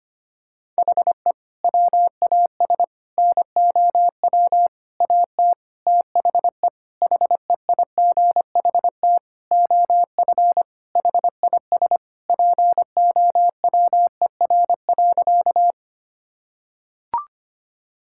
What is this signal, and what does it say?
Morse code, heard as a keyed tone, about 25 wpm, 700 Hz. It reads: HE WAS NOW AT THE HEIGHT OF HIS POWER.
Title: Sentences from Top 1600 Words 25wpm